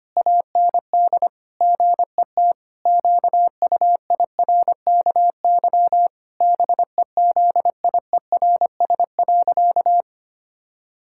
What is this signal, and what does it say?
Morse code, heard as a keyed tone, about 25 wpm, 700 Hz. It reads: AND GET QUIRKY BEZIERS.
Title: Pangram Challenge v2 - 25wpm